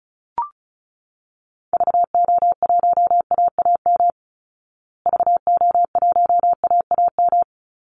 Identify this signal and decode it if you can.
Morse code, heard as a keyed tone, about 34 wpm, 700 Hz. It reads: VO1AAM VO1AAM
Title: Call Signs Repeated 3x - 35wpm